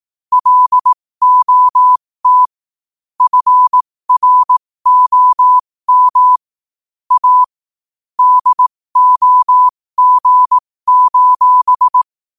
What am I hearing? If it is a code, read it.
Morse code, heard as a keyed tone, about 18 wpm, spaced as for 15 wpm, 1000 Hz. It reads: LOT FROM A DOG:
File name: qotdfiles / 15wpm